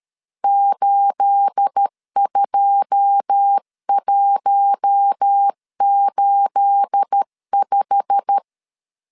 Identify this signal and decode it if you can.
Morse code, heard as a keyed tone, about 13 wpm, 800 Hz. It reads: 82185
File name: LEZIONI / 9 28 12wpm 2351